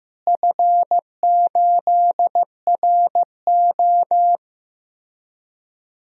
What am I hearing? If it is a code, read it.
Morse code, heard as a keyed tone, about 15 wpm, 700 Hz. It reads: F8RO
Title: DXCC-100 Call Signs - 4 Characters - Repeated 3x - 15wpm